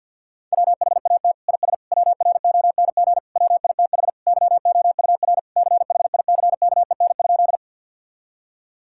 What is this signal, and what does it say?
Morse code, heard as a keyed tone, about 50 wpm, 690 Hz. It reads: WHAT IS WRONG WITH YOUR CHICKEN?
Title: Sentences from Top 1000 Words - Review Part 5 - 50wpm